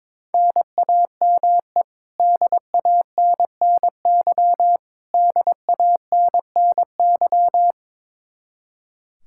Morse code, heard as a keyed tone, about 22 wpm, 700 Hz. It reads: NAME DANNY DANNY
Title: Practice QSO - Exchange 1 - Beginning - 22wpm